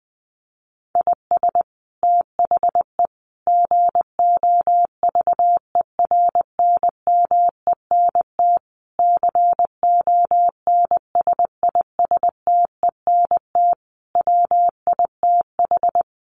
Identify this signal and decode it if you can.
Morse code, heard as a keyed tone, about 20 wpm, 700 Hz. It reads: IS THE GOVERNMENT CONSISTENT WITH
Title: Sentences from Top 1500 Words 20wpm